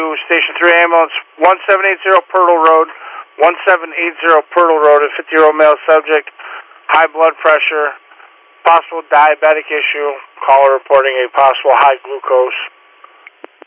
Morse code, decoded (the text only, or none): none